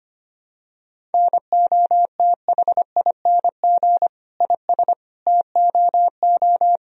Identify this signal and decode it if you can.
Morse code, heard as a keyed tone, about 25 wpm, 700 Hz. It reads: NOTHING IS TOO
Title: Sentences from Top 1000 Words - Review Part 1 - 25wpm